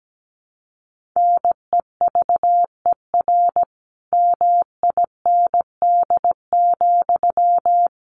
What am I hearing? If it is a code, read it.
Morse code, heard as a keyed tone, about 17 wpm, 700 Hz. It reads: NEVER MIND,
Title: Sentences from Top 1300 Words 17wpm